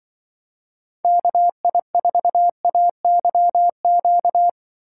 Morse code, heard as a keyed tone, about 24 wpm, 700 Hz. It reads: KI4AYQ
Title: Speed Racing - USA Call Signs 20wpm